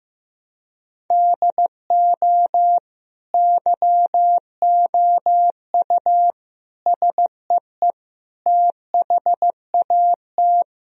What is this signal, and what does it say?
Morse code, heard as a keyed tone, about 15 wpm, 700 Hz. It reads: DO YOU SEE THAT
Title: Sentences from Top 1000 Words - Review Part 2 - 15wpm